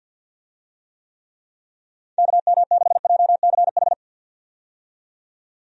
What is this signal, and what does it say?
Morse code, heard as a keyed tone, about 52 wpm, 700 Hz. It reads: KM6JCH
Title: USA Call Signs - 6 Characters - 50wpm